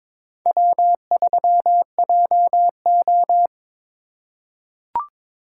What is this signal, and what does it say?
Morse code, heard as a keyed tone, about 22 wpm, 700 Hz. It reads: W3JO
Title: USA Call Signs - 4 Characters - 2x Character Spacing - 22wpm